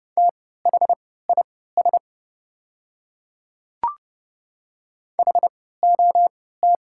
Morse code, heard as a keyed tone, about 30 wpm, spaced as for 17 wpm, 700 Hz. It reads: THIS HOT
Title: 3x Character Spacing - Top 100 Words - 30wpm